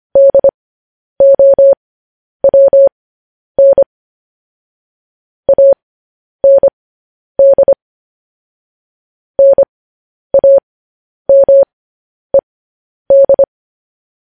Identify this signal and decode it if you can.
Morse code, heard as a keyed tone, about 25 wpm, spaced as for 10 wpm, 550 Hz. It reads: DOWN AND NAMED
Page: news-headlines-in-morse-code-at-10-wpm